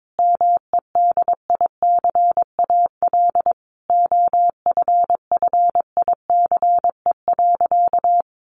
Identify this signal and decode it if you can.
Morse code, heard as a keyed tone, about 22 wpm, 700 Hz. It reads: MEDICAL OFFICE.
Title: Additional Easy Phrases 22wpm